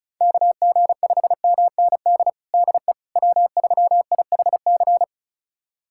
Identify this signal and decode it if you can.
Morse code, heard as a keyed tone, about 36 wpm, 700 Hz. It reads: KG5MND DE W3IHC
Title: USA Callsign DE Callsign 35wpm